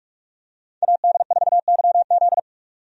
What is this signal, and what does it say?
Morse code, heard as a keyed tone, about 42 wpm, 700 Hz. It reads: AD4YZ
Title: USA Call Signs - 5 Characters - 45wpm